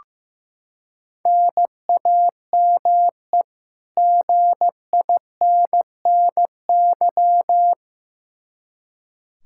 Morse code, heard as a keyed tone, about 15 wpm, 700 Hz. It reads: NAME GINNY
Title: Practice QSO - Exchange 1 - Beginning - 15wpm